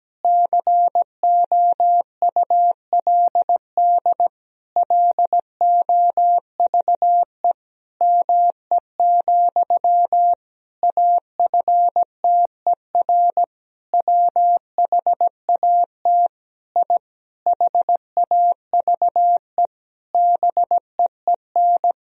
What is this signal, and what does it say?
Morse code, heard as a keyed tone, about 17 wpm, 700 Hz. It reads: COULD LOVE ME, AFTER WHAT I HAVE BEEN
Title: Sentences from Top 1000 Words - Review Part 1 - 17wpm